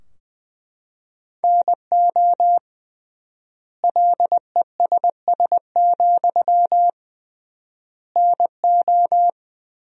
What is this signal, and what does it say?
Morse code, heard as a keyed tone, about 20 wpm, 700 Hz. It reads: NO LESS, NO
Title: 3x Word-Spacing - Sentences from Top 400 Words - 20wpm